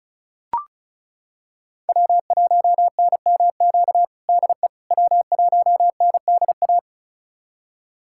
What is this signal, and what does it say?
Morse code, heard as a keyed tone, about 36 wpm, 700 Hz. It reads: W1NMQ DE W1NDA
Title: USA Callsign DE Callsign 35wpm